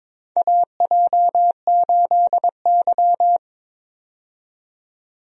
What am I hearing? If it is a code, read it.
Morse code, heard as a keyed tone, about 22 wpm, 700 Hz. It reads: AJ8Y
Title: USA Call Signs - 4 Characters - 22wpm